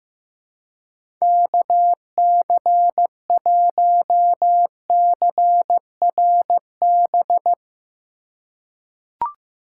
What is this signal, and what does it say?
Morse code, heard as a keyed tone, about 15 wpm, 700 Hz. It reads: KC1CRB
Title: USA Call Signs - 6 Characters - 3x Character Spacing - 15wpm